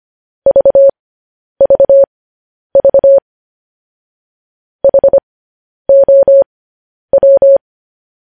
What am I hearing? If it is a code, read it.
Morse code, heard as a keyed tone, about 25 wpm, spaced as for 10 wpm, 550 Hz. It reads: VVV HOW
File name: podcast / News headlines-1x-25x10